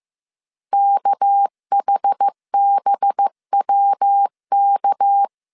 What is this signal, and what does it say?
Morse code, heard as a keyed tone, about 15 wpm, 800 Hz. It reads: KHBWK
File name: CW / DET / 1 6 1500C S B 14wpm 2958